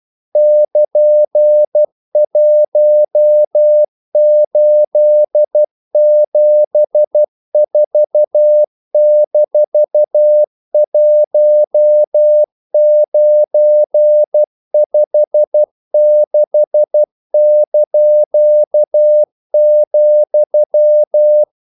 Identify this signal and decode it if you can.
Morse code, heard as a keyed tone, about 12 wpm, 600 Hz. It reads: <KN>1874-1956),
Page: English quotations in CW (telegraphy - morse)